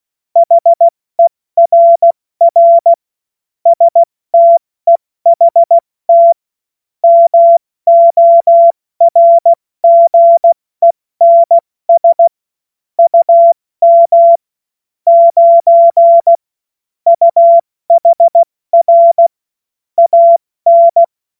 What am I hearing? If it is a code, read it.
Morse code, heard as a keyed tone, about 16 wpm, 680 Hz. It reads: HERR STEHT MORGENS UM 9 UHR AN
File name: Morsekurs / Witze / Witz 14.November 2023 70 F80